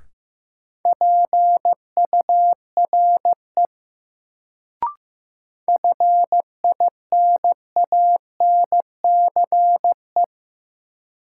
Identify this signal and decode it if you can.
Morse code, heard as a keyed tone, about 15 wpm, 700 Hz. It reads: PURE FINANCE
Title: Sentences from Top 1500 Words - New Word Drill - 15wpm